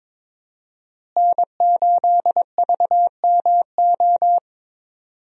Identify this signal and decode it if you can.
Morse code, heard as a keyed tone, about 21 wpm, 700 Hz. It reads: N8VMO
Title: USA Call Signs - 5 Characters - Repeated 3x - 22wpm